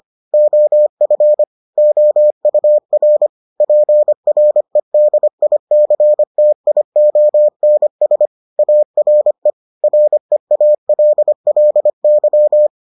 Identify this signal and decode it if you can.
Morse code, heard as a keyed tone, about 25 wpm, 600 Hz. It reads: OF OUR PREDICTIONS ARE REALLY